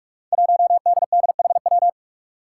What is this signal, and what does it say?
Morse code, heard as a keyed tone, about 46 wpm, 700 Hz. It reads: 1DDHW
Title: Five Challenging Letter-Number Combinations 45wpm